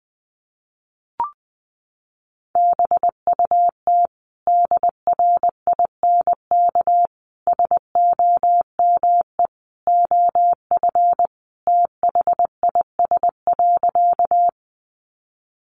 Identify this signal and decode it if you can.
Morse code, heard as a keyed tone, about 20 wpm, 700 Hz. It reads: BUT DRINK SOME OF THIS.
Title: Sentences from Top 1500 Words 20wpm